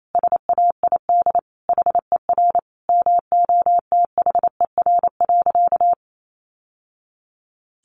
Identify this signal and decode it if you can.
Morse code, heard as a keyed tone, about 28 wpm, 700 Hz. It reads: SAID HER MOTHER.
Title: Sentences from Top 700 Words 28wpm